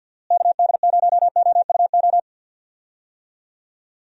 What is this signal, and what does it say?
Morse code, heard as a keyed tone, about 46 wpm, 700 Hz. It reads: KD0OUO
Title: USA Call Signs - 6 Characters - Repeated 3x - 50wpm